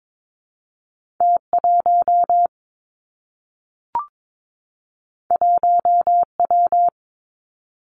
Morse code, heard as a keyed tone, about 22 wpm, 700 Hz. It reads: T 1 1 W
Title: Two Challenging Letter-Number Combinations 22wpm